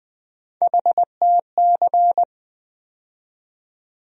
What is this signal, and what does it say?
Morse code, heard as a keyed tone, about 20 wpm, 700 Hz. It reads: HTC
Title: USA Call Sign Postfixes - 3 Letters 20wpm